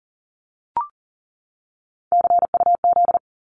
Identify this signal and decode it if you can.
Morse code, heard as a keyed tone, about 40 wpm, 700 Hz. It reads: CUZ
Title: Abbreviations 40wpm